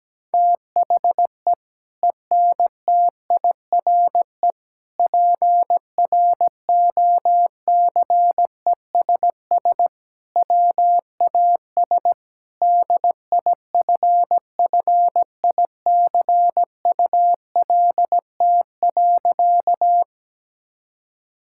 Morse code, heard as a keyed tone, about 17 wpm, 700 Hz. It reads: THE ENTIRE PROCESS WAS DIFFICULT.
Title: Sentences from Top 1000 Words - Review Part 5 - 17wpm